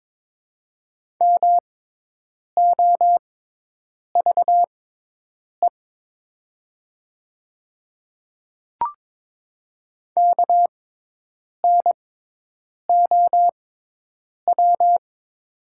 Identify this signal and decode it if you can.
Morse code, heard as a keyed tone, about 21 wpm, spaced as for 8 wpm, 700 Hz. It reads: MOVE KNOW
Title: 6x Character Spacing - Top 200 Words - 22wpm